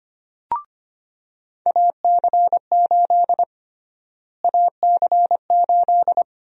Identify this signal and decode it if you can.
Morse code, heard as a keyed tone, about 25 wpm, 700 Hz. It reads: AC8 AC8
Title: USA Call Sign Prefixes - Repeated 2x - 25wpm